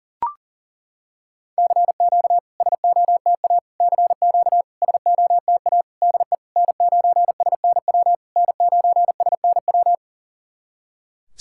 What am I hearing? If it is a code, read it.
Morse code, heard as a keyed tone, about 37 wpm, 690 Hz. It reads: CQ SOTA CQ SOTA DE N9SNW N9SNW
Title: SOTA QSO Practice 40wpm